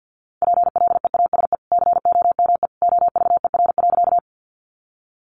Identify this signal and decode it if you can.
Morse code, heard as a keyed tone, about 50 wpm, 700 Hz. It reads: PLEASE COME OVER.
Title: Sentences from Top 1500 Words 50wpm